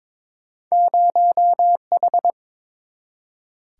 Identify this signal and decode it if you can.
Morse code, heard as a keyed tone, about 29 wpm, 700 Hz. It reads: 0H